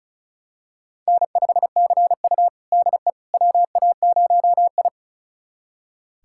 Morse code, heard as a keyed tone, about 36 wpm, 700 Hz. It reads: N5CU DE WA0I
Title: USA Callsign DE Callsign - 4 Characters - 35wpm